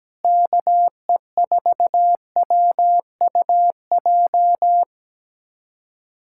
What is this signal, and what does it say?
Morse code, heard as a keyed tone, about 17 wpm, 700 Hz. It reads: KE4WUJ
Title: USA Callsign DE Callsign 17wpm